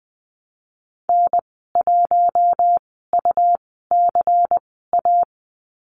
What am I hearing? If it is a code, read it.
Morse code, heard as a keyed tone, about 20 wpm, spaced as for 14 wpm, 700 Hz. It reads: N1UCA
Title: 167 Lesson 27g - Number 7 - Call Signs 20wpm